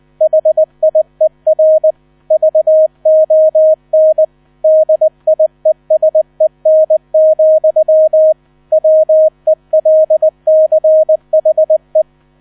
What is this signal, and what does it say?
Morse code, heard as a keyed tone, about 20 wpm, 630 Hz. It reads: HIER VON DIESEN, WELCHE
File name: 000 Vorwort 20wpm